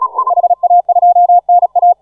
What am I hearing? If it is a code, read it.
Morse code, heard as a keyed tone, about 38 wpm, 720 Hz. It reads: HA2NA